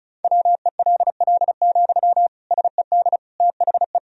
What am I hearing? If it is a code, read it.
Morse code, heard as a keyed tone, about 36 wpm, 700 Hz. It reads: WELL, SED THE
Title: Sentences from Top 400 Words and Abbreviations 35wpm